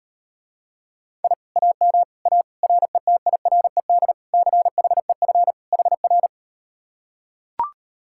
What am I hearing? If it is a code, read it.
Morse code, heard as a keyed tone, about 39 wpm, 700 Hz. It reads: I AM A RETIRED CHEF HR